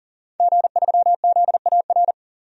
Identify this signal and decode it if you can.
Morse code, heard as a keyed tone, about 40 wpm, 700 Hz. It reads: G3ZAR
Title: DXCC-100 Call Signs - 5 Characters - 40wpm